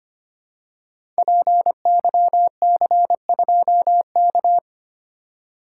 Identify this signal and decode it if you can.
Morse code, heard as a keyed tone, about 25 wpm, 700 Hz. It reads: PYC2K
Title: Code Groups - 5 Letter:Numbers - 25wpm